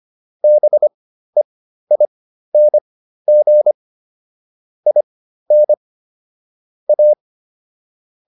Morse code, heard as a keyed tone, about 25 wpm, spaced as for 13 wpm, 600 Hz. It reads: BEING IN A